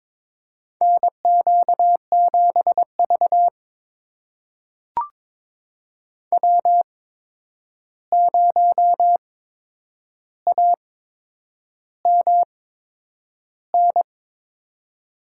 Morse code, heard as a keyed tone, about 22 wpm, spaced as for 9 wpm, 700 Hz. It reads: NQ7V W0AMN
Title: USA Call Signs - 8x Character Spacing - 22wpm